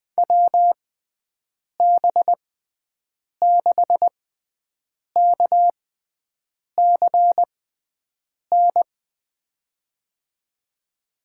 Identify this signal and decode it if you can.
Morse code, heard as a keyed tone, about 21 wpm, spaced as for 7 wpm, 700 Hz. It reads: WB6KCN
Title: USA Call Signs - 6x Character Spacing - 20wpm